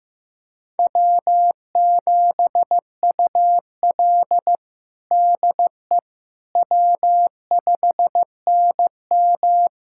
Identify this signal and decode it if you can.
Morse code, heard as a keyed tone, about 15 wpm, 700 Hz. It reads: W7UL DE W5NM
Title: USA Callsign DE Callsign - 4 Characters - 15wpm